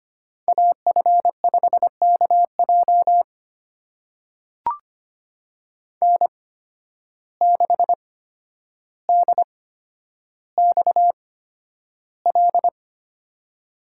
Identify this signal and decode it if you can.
Morse code, heard as a keyed tone, about 25 wpm, spaced as for 11 wpm, 700 Hz. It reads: AF5KJ N6DXL
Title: USA Call Signs - 5 Characters - 8x Character Spacing - 25wpm